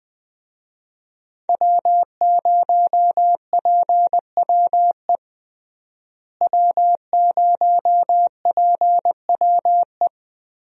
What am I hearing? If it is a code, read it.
Morse code, heard as a keyed tone, about 21 wpm, 700 Hz. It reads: W0PWE W0PWE
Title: USA Call Signs - 5 Characters - Repeated 3x - 20wpm